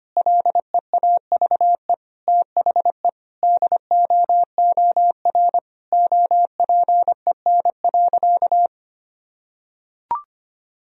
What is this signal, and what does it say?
Morse code, heard as a keyed tone, about 25 wpm, 700 Hz. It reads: LEAVE THE DOOR OPEN.
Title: Sentences from Top 1000 Words - Review Part 1 - 25wpm